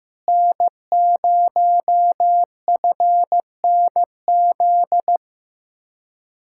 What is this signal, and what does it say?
Morse code, heard as a keyed tone, about 15 wpm, 700 Hz. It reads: N0FNZ
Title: Speed-Racing Common QSO Phrases Encore - 15wpm